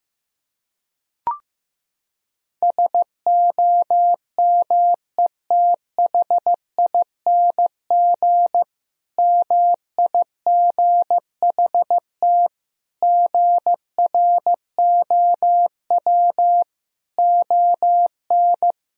Morse code, heard as a keyed tone, about 16 wpm, 700 Hz. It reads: SOMETHING MIGHT GROW ON
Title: Sentences from Top 1000 Words - Review Part 1 - 15wpm